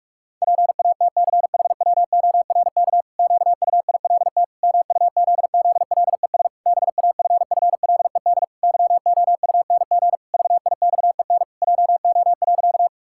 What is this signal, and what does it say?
Morse code, heard as a keyed tone, about 46 wpm, 700 Hz. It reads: PATCHWORK QUILT MUZZLES BAFFLED YOUNG VIXEN JO.
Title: Pangram Challenge v2 - 45wpm